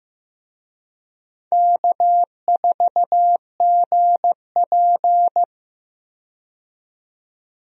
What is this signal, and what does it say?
Morse code, heard as a keyed tone, about 15 wpm, 700 Hz. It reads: K4GP
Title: USA Call Signs - 4 Characters - 15wpm